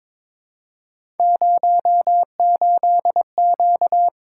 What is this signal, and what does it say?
Morse code, heard as a keyed tone, about 22 wpm, 700 Hz. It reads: 08Q
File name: practice / 22wpm / Three Letter-Number Combinations - 22wpm